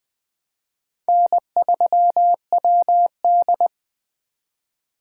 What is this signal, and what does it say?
Morse code, heard as a keyed tone, about 20 wpm, 700 Hz. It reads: N3WD
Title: 2023 POTA Activator Call Signs - 20wpm